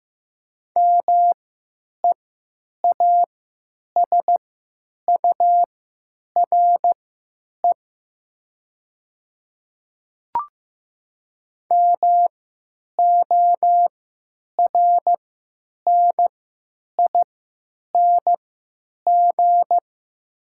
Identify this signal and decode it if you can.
Morse code, heard as a keyed tone, about 15 wpm, spaced as for 9 wpm, 700 Hz. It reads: MEASURE MORNING
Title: 3x Character Spacing - Top 500 Words - 15wpm